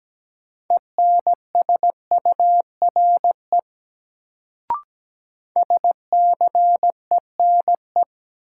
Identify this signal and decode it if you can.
Morse code, heard as a keyed tone, about 17 wpm, 700 Hz. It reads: ENSURE SCENE
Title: Sentences from Top 1100 Words - New Word Drill Alt - 17wpm